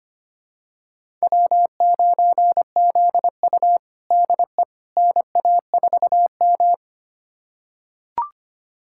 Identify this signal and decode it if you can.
Morse code, heard as a keyed tone, about 25 wpm, 700 Hz. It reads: W9ZU DE NA4M
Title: USA Callsign DE Callsign - 4 Characters - 25wpm